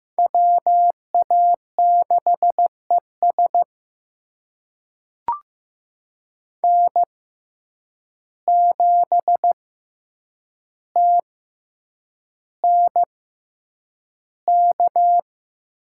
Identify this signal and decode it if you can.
Morse code, heard as a keyed tone, about 15 wpm, spaced as for 8 wpm, 700 Hz. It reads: WA6ES N7TNK
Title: USA Call Signs - 6x Character Spacing - 15wpm